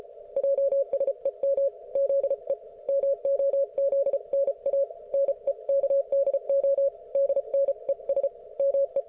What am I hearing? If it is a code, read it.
Morse code, heard as a keyed tone, about 34 wpm, 550 Hz. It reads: JSEM ZE MOZNA NEKDO DNES ME